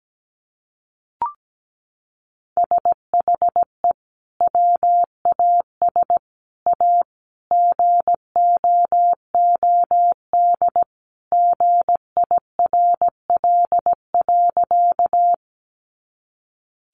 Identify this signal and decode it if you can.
Morse code, heard as a keyed tone, about 17 wpm, 700 Hz. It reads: SHE WAS A GOOD GIRL.
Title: Sentences from Top 400 Words 17wpm